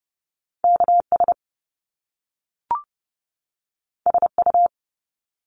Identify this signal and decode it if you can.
Morse code, heard as a keyed tone, about 30 wpm, 700 Hz. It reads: KS SU
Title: Top Two Letter Combinations in Words 30wpm